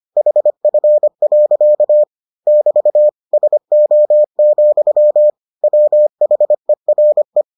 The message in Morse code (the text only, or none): HF. <BT> SO, WHERE